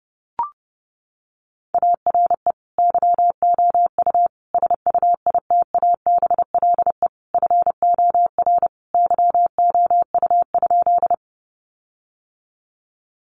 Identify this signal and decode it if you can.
Morse code, heard as a keyed tone, about 30 wpm, 700 Hz. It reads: ARE YOU SUITABLE FOR YOU?
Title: Sentences from Top 1400 Words 30wpm